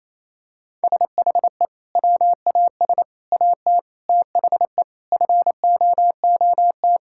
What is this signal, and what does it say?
Morse code, heard as a keyed tone, about 28 wpm, 700 Hz. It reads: SHE WAS AT THE FOOT
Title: Sentences from Top 1000 Words - Review Part 5 - 28wpm